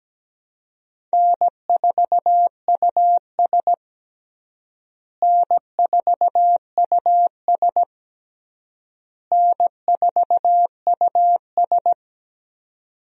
Morse code, heard as a keyed tone, about 17 wpm, 700 Hz. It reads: N4US N4US N4US
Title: USA Call Signs - 4 Characters - Repeated 3x - 17wpm